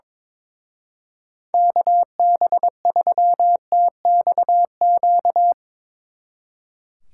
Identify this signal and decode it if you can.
Morse code, heard as a keyed tone, about 24 wpm, 700 Hz. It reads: KB3TXQ
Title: Speed Racing - USA 2x3 Call Signs 22wpm